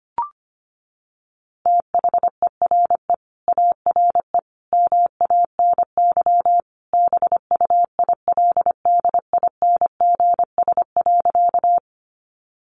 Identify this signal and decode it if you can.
Morse code, heard as a keyed tone, about 25 wpm, 700 Hz. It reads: THERE ARE MANY BUILDINGS.
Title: Easy Phrases 25wpm